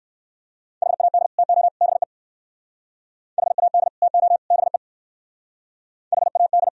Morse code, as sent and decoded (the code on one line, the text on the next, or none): .... .- -.. / - --- / -... / .... .- -.. / - --- / -... / .... .- -..
HAD TO BE HAD TO BE HAD